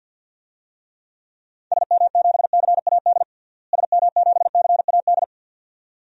Morse code, heard as a keyed tone, about 52 wpm, 700 Hz. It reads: SM7CAD SM7CAD
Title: DXCC-100 Call Signs - 6 Characters - Repeated 2x - 50wpm